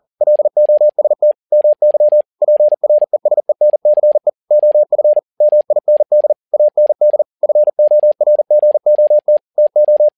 Morse code, heard as a keyed tone, about 40 wpm, 600 Hz. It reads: LOST MY PRESENCE OF MIND AND FORGOT TO
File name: learning center / tower trouble / tower touble 40wpm